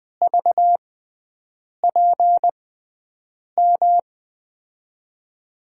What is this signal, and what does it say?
Morse code, heard as a keyed tone, about 20 wpm, spaced as for 7 wpm, 700 Hz. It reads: VPM
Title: USA Call Signs - 6 Characters - 6x Character Spacing - 20wpm